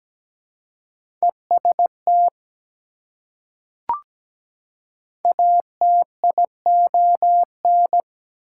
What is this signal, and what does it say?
Morse code, heard as a keyed tone, about 17 wpm, 700 Hz. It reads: EST ATION
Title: Prefixes and Suffixes 17wpm